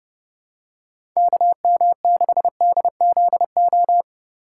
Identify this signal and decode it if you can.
Morse code, heard as a keyed tone, about 30 wpm, 700 Hz. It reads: KM6DZO
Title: USA Call Signs - 6 Characters - 4x Character Spacing - 30wpm